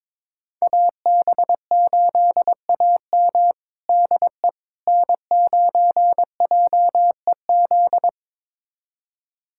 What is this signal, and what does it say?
Morse code, heard as a keyed tone, about 21 wpm, 700 Hz. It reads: AB8AM DE N9JEZ
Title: USA Callsign DE Callsign - 5 Characters - 22wpm